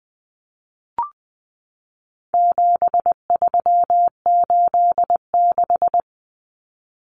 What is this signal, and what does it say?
Morse code, heard as a keyed tone, about 20 wpm, 700 Hz. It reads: 7386
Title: Common Cut Numbers - 4-Digit 20wpm